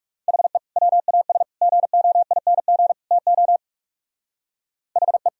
Morse code, heard as a keyed tone, about 45 wpm, 700 Hz. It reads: HE WAS GOING TO HE